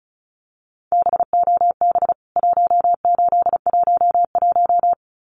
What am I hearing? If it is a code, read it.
Morse code, heard as a keyed tone, about 34 wpm, 700 Hz. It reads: BOB 1811
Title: CWT Exchange 35wpm